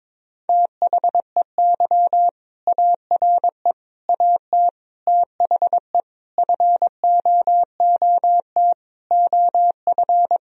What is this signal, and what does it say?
Morse code, heard as a keyed tone, about 21 wpm, 700 Hz. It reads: THEY ARE AT THE FOOT OF
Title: Sentences from Top 1000 Words - Review Part 4 - 22wpm